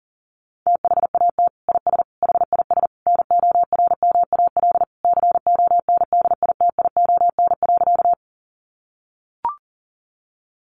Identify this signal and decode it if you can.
Morse code, heard as a keyed tone, about 40 wpm, 700 Hz. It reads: THAT IS HIS NORMAL CONDITION.
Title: Sentences from Top 1200 Words 40wpm